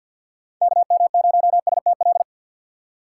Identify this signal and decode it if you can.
Morse code, heard as a keyed tone, about 47 wpm, 700 Hz. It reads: KM0STL
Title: USA Call Signs - 6x Character Spacing - 50wpm